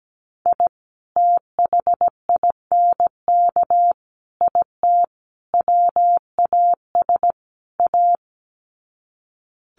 Morse code, heard as a keyed tone, about 17 wpm, 700 Hz. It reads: I THINK IT WAS A
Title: Sets of 5 Words 17wpm